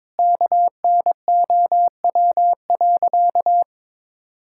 Morse code, heard as a keyed tone, about 22 wpm, 700 Hz. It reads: KNOW.
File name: practice / 22wpm / Sentences with 3-Word Phrases - 22wpm